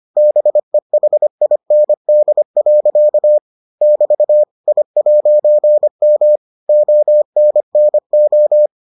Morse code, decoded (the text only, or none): BEHIND. <BT> I'M ONNO